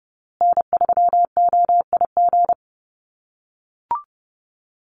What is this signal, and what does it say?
Morse code, heard as a keyed tone, about 30 wpm, 700 Hz. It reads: N3OIG
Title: USA Call Signs Repeated 2x 30wpm